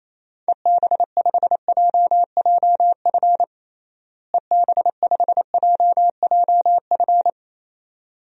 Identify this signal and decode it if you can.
Morse code, heard as a keyed tone, about 28 wpm, 700 Hz. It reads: EB5JJF EB5JJF
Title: DXCC-100 Call Signs - 6 Characters - Repeated 2x - 28wpm